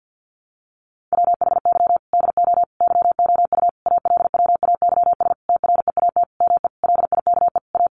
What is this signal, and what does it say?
Morse code, heard as a keyed tone, about 50 wpm, 700 Hz. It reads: WHY DO YOU ALWAYS TREAT ME LIKE A
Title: Sentences from Top 1300 Words 50wpm